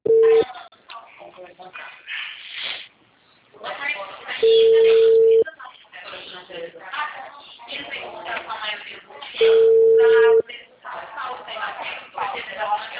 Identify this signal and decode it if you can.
Ringback tone (China, 450 Hz), with speech.
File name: out- -2025- - - .6786